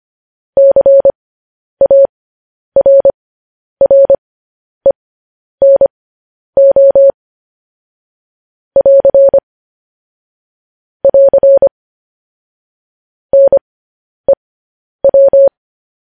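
Morse code, heard as a keyed tone, about 25 wpm, spaced as for 10 wpm, 550 Hz. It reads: CARRENO <AR> <AR> NEW
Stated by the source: Morse code news headlines broadcast.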